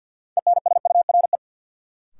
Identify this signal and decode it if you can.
Morse code, heard as a keyed tone, about 47 wpm, 700 Hz. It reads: ENSURE
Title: Sentences from Top 1100 Words - New Word Drill Alt - 50wpm